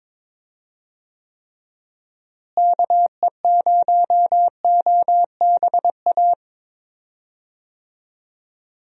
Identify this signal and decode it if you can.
Morse code, heard as a keyed tone, about 22 wpm, 700 Hz. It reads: KE0OBA
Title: USA Call Signs - 6 Characters - Repeated 2x - 22wpm